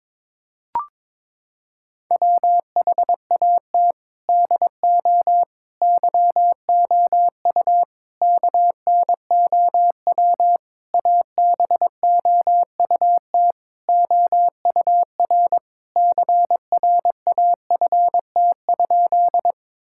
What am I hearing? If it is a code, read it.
Morse code, heard as a keyed tone, about 22 wpm, 700 Hz. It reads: WHAT DO YOU KNOW ABOUT OUR CRAFT?
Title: Sentences from Top 1000 Words - Review Part 6 - 22wpm